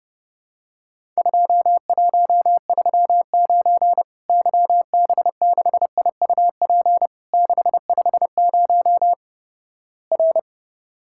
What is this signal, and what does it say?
Morse code, heard as a keyed tone, about 30 wpm, 700 Hz. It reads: 2139 YB6IUP 650 R
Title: Worked All Europe DX Contest - QTC Exchange - 30wpm